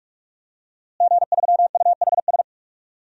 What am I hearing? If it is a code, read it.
Morse code, heard as a keyed tone, about 44 wpm, 700 Hz. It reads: G3UHS
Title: DXCC-100 Call Signs - 5 Characters - Repeated 2x - 45wpm